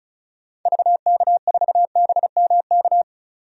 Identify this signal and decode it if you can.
Morse code, heard as a keyed tone, about 36 wpm, 700 Hz. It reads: VK4BMK
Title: DXCC-100 Call Signs - 6 Characters - 35wpm